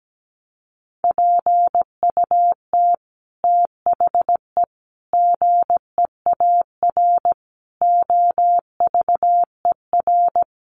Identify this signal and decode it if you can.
Morse code, heard as a keyed tone, about 17 wpm, 700 Hz. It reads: PUT THE GEAR OVER
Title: Sentences from Top 1400 Words 17wpm